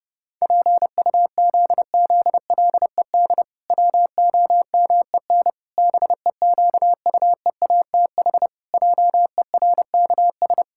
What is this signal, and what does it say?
Morse code, heard as a keyed tone, about 30 wpm, 700 Hz. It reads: PUZZLED WOMEN BEQUEATH JERKS
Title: Pangram Challenge v2 - 30wpm